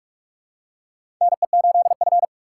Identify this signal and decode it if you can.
Morse code, heard as a keyed tone, about 41 wpm, 700 Hz. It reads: NE8F
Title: USA Call Signs - 4 Characters - 2x Character Spacing - 45wpm